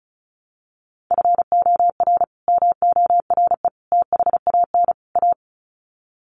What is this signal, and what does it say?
Morse code, heard as a keyed tone, about 34 wpm, 700 Hz. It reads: FOR MORE THAN A
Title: Sets of 4 Words 35wpm